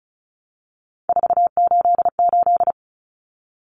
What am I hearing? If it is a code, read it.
Morse code, heard as a keyed tone, about 34 wpm, 700 Hz. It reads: 488
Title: Numbers - 3-Digit 35wpm